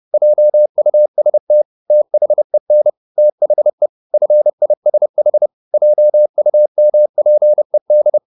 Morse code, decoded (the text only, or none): JUST THEN THE FISH JUMPED